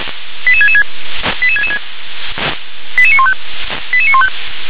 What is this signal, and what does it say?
Five-tone selective call (ZVEI), heard as quick runs of five tones, tones of about 70 ms. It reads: 88666 88666 88915 88915